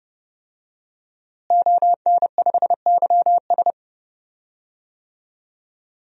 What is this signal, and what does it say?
Morse code, heard as a keyed tone, about 30 wpm, 700 Hz. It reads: ON5YS
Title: DXCC-100 Call Signs - 5 Characters - Repeated 3x - 30wpm